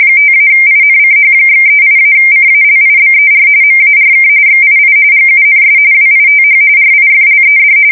RTTY, about 45.45 baud. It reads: 5-INCH/38-CALIBER MARK XII GUNS WITH RIFLED
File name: fsk170 / USS-New-Jersey